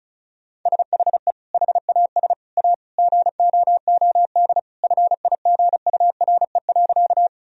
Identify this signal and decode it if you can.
Morse code, heard as a keyed tone, about 36 wpm, 700 Hz. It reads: SHE HAS A GOOD FIGURE.
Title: Sentences from Top 1000 Words - Review Part 2 - 35wpm